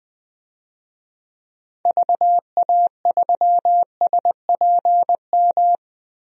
Morse code, heard as a keyed tone, about 20 wpm, 700 Hz. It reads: VA3SPM
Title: DXCC-100 Call Signs - 6 Characters - Repeated 3x - 20wpm